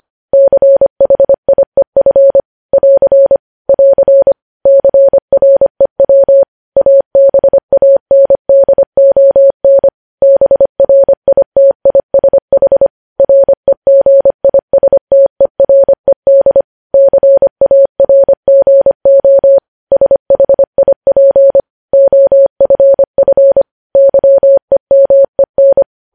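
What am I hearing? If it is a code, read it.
Morse code, heard as a keyed tone, about 25 wpm, 550 Hz. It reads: CHIEF <AR> <AR> CREW ABANDON BRITISH REGISTERED CARGO SHIP OFF YEMEN
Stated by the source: Morse code news headlines sent at 25, opening with vvv.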